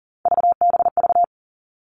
Morse code, heard as a keyed tone, about 40 wpm, 700 Hz. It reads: VBV
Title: All Cut Numbers - 3-Digit 40wpm